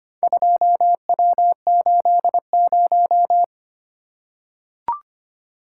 Morse code, heard as a keyed tone, about 25 wpm, 700 Hz. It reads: 2W80
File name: practice / 25wpm / Four Letter-Number Combinations - 25wpm